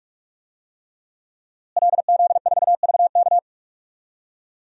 Morse code, heard as a keyed tone, about 46 wpm, 700 Hz. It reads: LZ4VK
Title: DXCC-100 Call Signs - 5 Characters - Repeated 3x - 45wpm